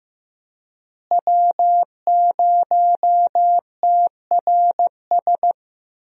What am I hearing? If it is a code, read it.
Morse code, heard as a keyed tone, about 15 wpm, 700 Hz. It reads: W0TRS
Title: USA Call Signs - 5 Characters - Repeated 2x - 15wpm